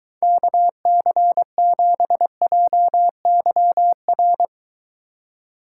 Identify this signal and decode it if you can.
Morse code, heard as a keyed tone, about 23 wpm, 700 Hz. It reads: KC7JYR